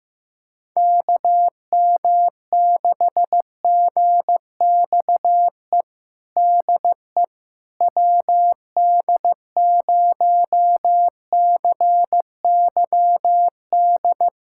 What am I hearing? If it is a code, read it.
Morse code, heard as a keyed tone, about 15 wpm, 700 Hz. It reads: KM6GXE DE WD0CYD
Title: USA Callsign DE Callsign - 6 Characters - 15wpm